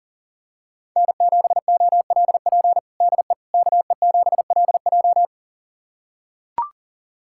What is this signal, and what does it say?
Morse code, heard as a keyed tone, about 37 wpm, 700 Hz. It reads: N7OLP DE KE7LJ
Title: USA Callsign DE Callsign - 5 Characters - 40wpm